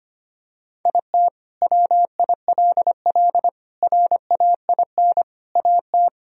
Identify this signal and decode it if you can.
Morse code, heard as a keyed tone, about 25 wpm, 700 Hz. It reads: IT WILL RAIN AT
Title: Sentences from Top 1000 Words - Review Part 1 - 25wpm